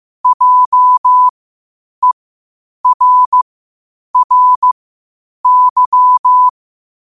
Morse code, heard as a keyed tone, about 15 wpm, spaced as for 8 wpm, 1000 Hz. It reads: JERRY
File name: qotdfiles / 8wpm